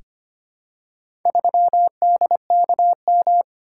Morse code, heard as a keyed tone, about 25 wpm, 700 Hz. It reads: 3DKM
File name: practice / 25wpm / Four Letter-Number Combinations - 25wpm